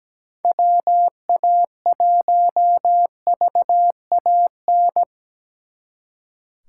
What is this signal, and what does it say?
Morse code, heard as a keyed tone, about 20 wpm, 700 Hz. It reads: WA1VAN
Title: Speed Racing - USA 2x3 Call Signs 17wpm